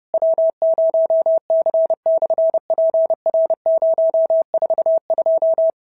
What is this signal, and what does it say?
Morse code, heard as a keyed tone, about 30 wpm, 650 Hz. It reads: W0C/PR042
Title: SOTA QSO Practice 30wpm